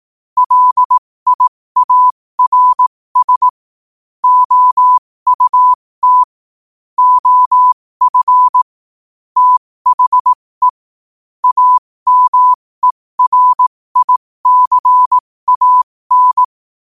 Morse code, heard as a keyed tone, about 18 wpm, spaced as for 15 wpm, 1000 Hz. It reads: LIARS OUT OF THE AMERICAN
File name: qotdfiles / 15wpm